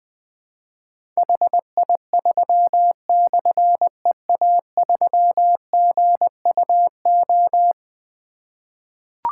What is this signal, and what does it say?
Morse code, heard as a keyed tone, about 20 wpm, 700 Hz. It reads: HI3/EA3GUO
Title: Difficult Call Signs 20wpm